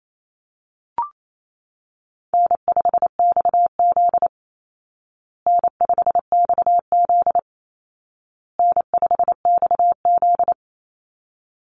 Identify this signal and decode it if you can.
Morse code, heard as a keyed tone, about 28 wpm, 700 Hz. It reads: N5XZ N5XZ N5XZ
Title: Easy Call Signs Repeated 3x - 28wpm